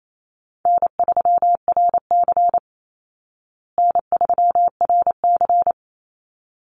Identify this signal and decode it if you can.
Morse code, heard as a keyed tone, about 28 wpm, 700 Hz. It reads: N3RC N3RC
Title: Call Signs Repeated 3x - 28wpm